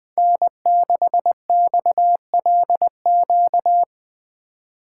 Morse code, heard as a keyed tone, about 20 wpm, 700 Hz. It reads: N6XLQ